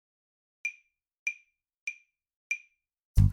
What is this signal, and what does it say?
There is no code.